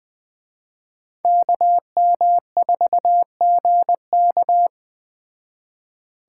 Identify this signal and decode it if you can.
Morse code, heard as a keyed tone, about 19 wpm, 700 Hz. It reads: KM4GK